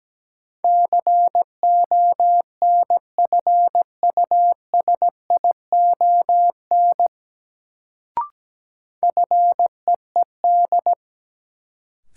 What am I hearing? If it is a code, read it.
Morse code, heard as a keyed tone, about 17 wpm, 700 Hz. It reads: CONFUSION FEED
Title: Sentences from Top 1600 Words - New Word Drill Alt - 17wpm